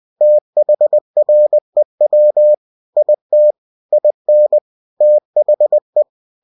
Morse code, heard as a keyed tone, about 20 wpm, 600 Hz. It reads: THREW IT IN THE
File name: cw-books / winnie-the-pooh / winnie-20wpm-page-0092